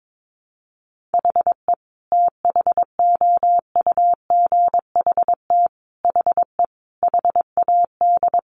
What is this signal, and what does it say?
Morse code, heard as a keyed tone, about 22 wpm, 700 Hz. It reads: HE THOUGHT HE HAD
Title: Sentences from Top 600 Words 22wpm